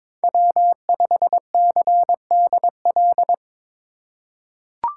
Morse code, heard as a keyed tone, about 22 wpm, 700 Hz. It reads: W5CDL
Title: USA Call Signs - 5 Characters - 3x Character Spacing - 22wpm